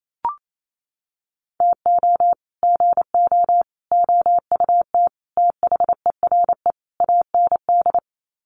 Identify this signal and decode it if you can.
Morse code, heard as a keyed tone, about 28 wpm, 700 Hz. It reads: TO GO OUT THERE AND
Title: Sets of 5 Words Encore2 28wpm